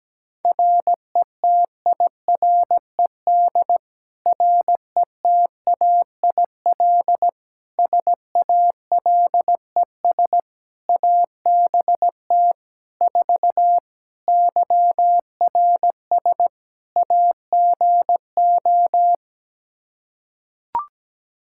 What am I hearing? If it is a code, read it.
Morse code, heard as a keyed tone, about 17 wpm, 700 Hz. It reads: RETIRED RETAIL SALES ABT 4 YRS AGO
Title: Practice QSO - Exchange 3 - More Info - 17wpm